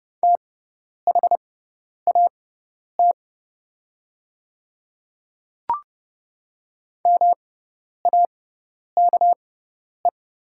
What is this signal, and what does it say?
Morse code, heard as a keyed tone, about 30 wpm, spaced as for 10 wpm, 700 Hz. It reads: THAT MAKE
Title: 6x Character Spacing - Top 300 Words - 30wpm